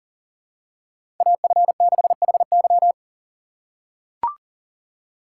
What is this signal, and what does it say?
Morse code, heard as a keyed tone, about 40 wpm, 700 Hz. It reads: AF6HY
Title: USA Call Signs - 5 Characters - 4x Character Spacing - 40wpm